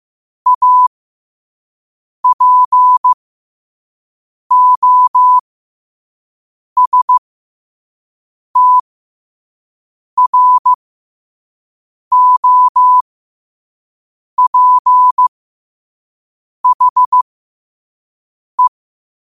Morse code, heard as a keyed tone, about 15 wpm, spaced as for 5 wpm, 1000 Hz. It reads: APOSTROPHE